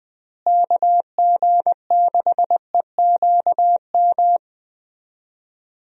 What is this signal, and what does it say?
Morse code, heard as a keyed tone, about 20 wpm, 700 Hz. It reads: KG6EQM
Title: USA Call Signs - 6 Characters - Repeated 3x - 20wpm